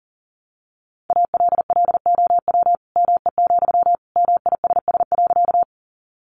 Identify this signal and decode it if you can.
Morse code, heard as a keyed tone, about 40 wpm, 700 Hz. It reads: ALLOW ME, MISS.
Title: Sentences from Top 700 Words 40wpm